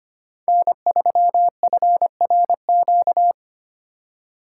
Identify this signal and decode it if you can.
Morse code, heard as a keyed tone, about 25 wpm, 700 Hz. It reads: N3FRQ
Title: USA Call Signs - 4x Character Spacing - 25wpm